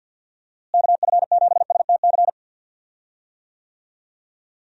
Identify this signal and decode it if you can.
Morse code, heard as a keyed tone, about 46 wpm, 700 Hz. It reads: KF7STC
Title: USA Call Signs - 6 Characters - Repeated 3x - 50wpm